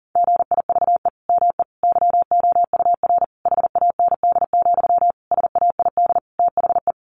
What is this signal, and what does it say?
Morse code, heard as a keyed tone, about 40 wpm, 700 Hz. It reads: GIVE ME YOUR HAND, SAID THE
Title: Sentences from Top 1500 Words 40wpm